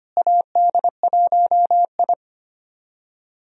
Morse code, heard as a keyed tone, about 25 wpm, 700 Hz. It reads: AD1I
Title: USA Call Signs - 4 Characters - Repeated 2x - 25wpm